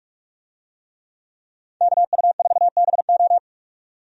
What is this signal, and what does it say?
Morse code, heard as a keyed tone, about 42 wpm, 700 Hz. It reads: KU4BO
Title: USA Call Signs - 5 Characters - 45wpm